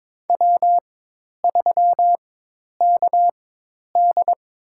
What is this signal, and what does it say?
Morse code, heard as a keyed tone, about 22 wpm, spaced as for 10 wpm, 700 Hz. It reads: W3KD
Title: USA Call Signs - 4 Characters - 4x Character Spacing - 22wpm